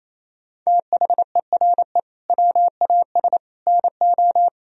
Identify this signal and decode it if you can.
Morse code, heard as a keyed tone, about 28 wpm, 700 Hz. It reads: THERE WAS NO
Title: Sentences from Top 1000 Words - Review Part 1 - 28wpm